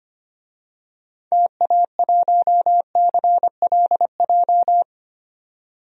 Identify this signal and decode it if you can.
Morse code, heard as a keyed tone, about 25 wpm, 700 Hz. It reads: TA1CLJ
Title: DXCC-100 Call Signs - 6 Characters - 25wpm